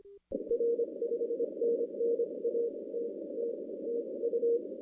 Morse code, heard as a keyed tone, about 26 wpm, spaced as for 21 wpm, 470 Hz. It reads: R 5NN ATT TU
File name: 2016 arrl dx cw / 2016-ARRL-DX-CW-K5TR- - 0-26